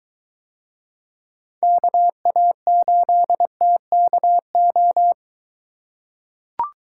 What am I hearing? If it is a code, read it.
Morse code, heard as a keyed tone, about 22 wpm, 700 Hz. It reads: KA8TKO